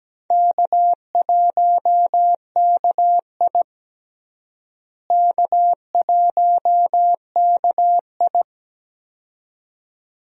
Morse code, heard as a keyed tone, about 17 wpm, 700 Hz. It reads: K1KI K1KI
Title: USA Call Signs - 4 Characters - Repeated 2x - 17wpm